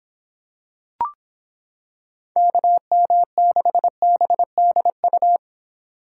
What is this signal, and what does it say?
Morse code, heard as a keyed tone, about 26 wpm, 700 Hz. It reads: KM6BDU